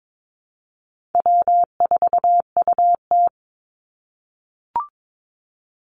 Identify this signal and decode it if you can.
Morse code, heard as a keyed tone, about 22 wpm, 700 Hz. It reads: W4UT
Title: Call Signs - 22wpm